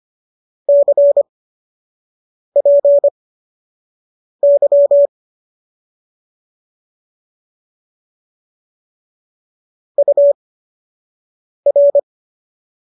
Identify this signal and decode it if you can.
Morse code, heard as a keyed tone, about 25 wpm, spaced as for 6 wpm, 570 Hz. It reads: CPY UR
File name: qso012 06